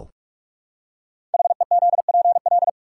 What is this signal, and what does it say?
Morse code, heard as a keyed tone, about 44 wpm, 700 Hz. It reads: HEZPL
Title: Code Groups - 5 Letters - 45wpm